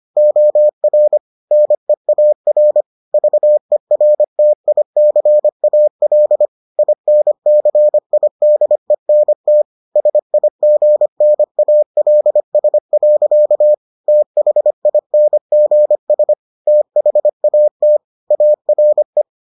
Morse code, heard as a keyed tone, about 25 wpm, 600 Hz. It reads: OR NEAR VERTICAL INCIDENT SIGNALS. THINGS THAT ARE